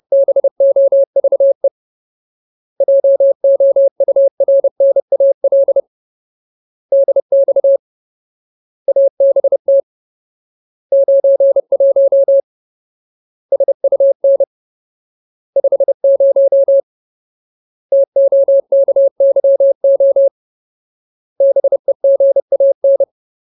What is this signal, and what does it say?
Morse code, heard as a keyed tone, about 30 wpm, 550 Hz. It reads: BOVE JOURNAL DX ABT 91 SUN 50 TOKYO BEGAN
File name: Ultimate Word List / 30WPM Ultimate